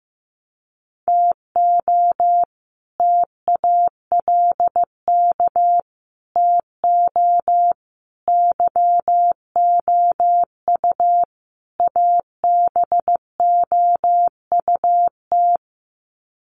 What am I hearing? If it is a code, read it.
Morse code, heard as a keyed tone, about 15 wpm, 700 Hz. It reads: TO TALK TO YOU ABOUT
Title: Sets of 5 Words Encore 15wpm